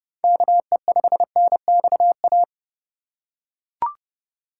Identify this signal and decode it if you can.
Morse code, heard as a keyed tone, about 30 wpm, 700 Hz. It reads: KE5NXA
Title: USA Call Signs - 2x Character Spacing - 30wpm